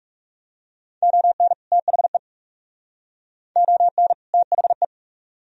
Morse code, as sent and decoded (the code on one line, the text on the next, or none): --- -. / - .... / --- -. / - .... .
ON THE ON THE